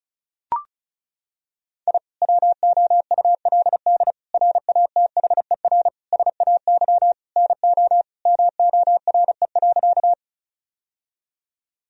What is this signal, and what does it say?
Morse code, heard as a keyed tone, about 34 wpm, 700 Hz. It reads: I WOULD RATHER SAY NO MORE.
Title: Sentences from Top 1000 Words - Review Part 4 - 35wpm